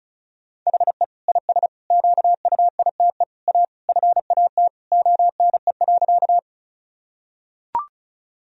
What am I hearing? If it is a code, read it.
Morse code, heard as a keyed tone, about 34 wpm, 700 Hz. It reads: HE IS QUITE A FAT ONE.
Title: Sentences from Top 1000 Words - Review Part 4 - 35wpm